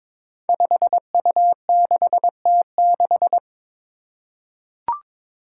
Morse code, heard as a keyed tone, about 22 wpm, 700 Hz. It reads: 5U6T6
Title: Five Challenging Letter-Number Combinations 22wpm